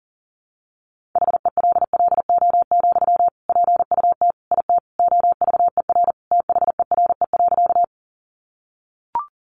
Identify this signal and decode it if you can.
Morse code, heard as a keyed tone, about 40 wpm, 700 Hz. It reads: HELLO, PUT IT OVER THERE.
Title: Sentences from Top 1100 Words 40wpm